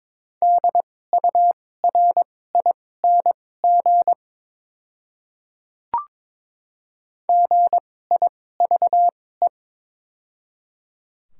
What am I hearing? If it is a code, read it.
Morse code, heard as a keyed tone, about 22 wpm, spaced as for 16 wpm, 700 Hz. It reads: DURING GIVE
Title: 2x Character Spacing - Top 500 Words - 22wpm